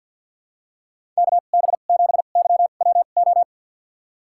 Morse code, heard as a keyed tone, about 52 wpm, spaced as for 36 wpm, 700 Hz. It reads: KB7YWO
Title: USA Call Signs - 2x Character Spacing - 50wpm